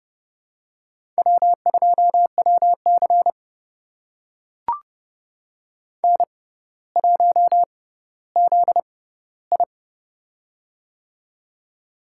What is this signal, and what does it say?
Morse code, heard as a keyed tone, about 30 wpm, spaced as for 15 wpm, 700 Hz. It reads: W2WC N1ZI
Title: USA Call Signs - 6x Character Spacing - 30wpm